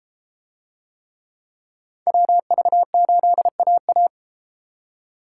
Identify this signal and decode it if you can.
Morse code, heard as a keyed tone, about 32 wpm, 700 Hz. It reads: WV8AA